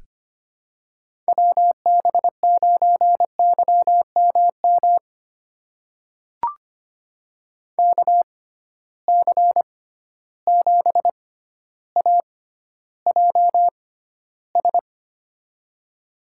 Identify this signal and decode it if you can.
Morse code, heard as a keyed tone, about 25 wpm, spaced as for 13 wpm, 700 Hz. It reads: WB9YMM KC7AJS
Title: USA Call Signs - 6 Characters - 6x Character Spacing - 25wpm